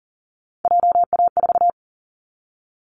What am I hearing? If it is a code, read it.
Morse code, heard as a keyed tone, about 42 wpm, 700 Hz. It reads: JA4
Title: Callsign Prefixes for Top 50 DXCC - 40wpm